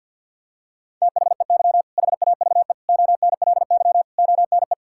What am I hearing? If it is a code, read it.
Morse code, heard as a keyed tone, about 48 wpm, 700 Hz. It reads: THEY HAVE ONLY ONE
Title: Sentences from Top 1600 Words 50wpm